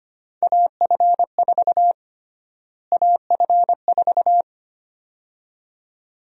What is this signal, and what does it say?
Morse code, heard as a keyed tone, about 25 wpm, 700 Hz. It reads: AF4 AF4
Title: USA Call Sign Prefixes - Repeated 2x - 25wpm